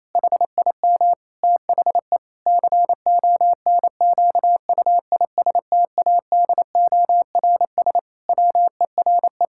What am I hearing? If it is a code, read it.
Morse code, heard as a keyed tone, about 27 wpm, 700 Hz. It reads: HIM THE CONQUISTADORS WERE
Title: Pangram Challenge v2 - 28wpm